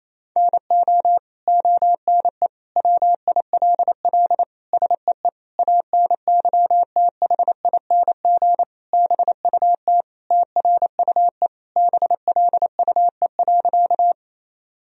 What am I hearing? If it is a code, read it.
Morse code, heard as a keyed tone, about 28 wpm, 700 Hz. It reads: NO ONE WILL SEE ANYTHING BUT TRUE BLUE.
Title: Sentences from Top 1000 Words - Review Part 3 - 28wpm